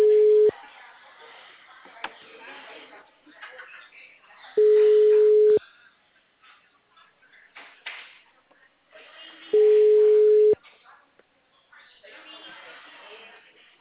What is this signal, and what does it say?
Ringback tone (Europe, 425 Hz), with speech.
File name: out- -2003- - - .80697